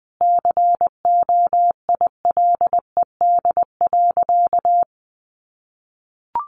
Morse code, heard as a keyed tone, about 20 wpm, 700 Hz. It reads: COILED.
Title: A Sound Of Thunder 20wpm